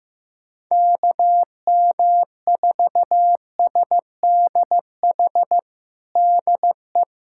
Morse code, heard as a keyed tone, about 15 wpm, 700 Hz. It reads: KM4SDH DE
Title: USA Callsign DE Callsign 15wpm